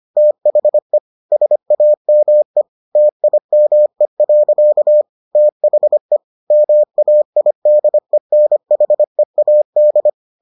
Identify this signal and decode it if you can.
Morse code, heard as a keyed tone, about 25 wpm, 600 Hz. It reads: THE SAME TIME. THE MAIDENHEAD